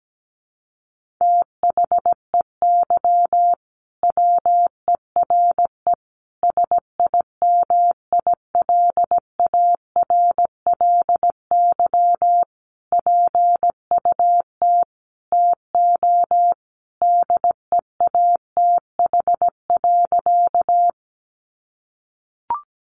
Morse code, heard as a keyed tone, about 17 wpm, 700 Hz. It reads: THEY WERE SIMILARLY PUT TO DEATH.
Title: Sentences from Top 1400 Words 17wpm